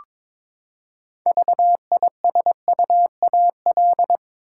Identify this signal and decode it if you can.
Morse code, heard as a keyed tone, about 22 wpm, 700 Hz. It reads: VISUAL
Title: Six Letter Words - 22wpm